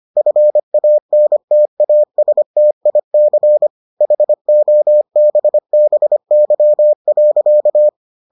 Morse code, heard as a keyed tone, about 25 wpm, 600 Hz. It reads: FANTASTIC HOBBY.